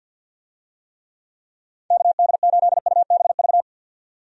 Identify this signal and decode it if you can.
Morse code, heard as a keyed tone, about 50 wpm, 700 Hz. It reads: KD8UBV
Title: USA Call Signs - 6 Characters - 50wpm